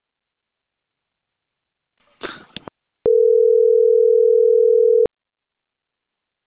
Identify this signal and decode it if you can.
Ringback tone (North America), 440 + 480 Hz.